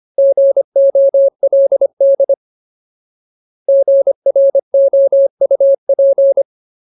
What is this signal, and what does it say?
Morse code, heard as a keyed tone, about 25 wpm, 550 Hz. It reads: GOLD GROUP
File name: Ultimate Word List / 25WPM Ultimate